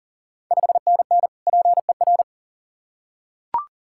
Morse code, heard as a keyed tone, about 40 wpm, 690 Hz. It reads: 5NN PER
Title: Ontario QSO Party Exchange 40wpm